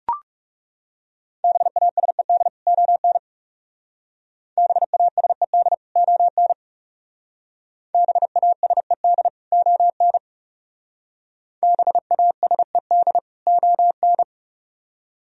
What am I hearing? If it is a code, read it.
Morse code, heard as a keyed tone, about 38 wpm, 700 Hz. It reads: BASED ON BASED ON BASED ON BASED ON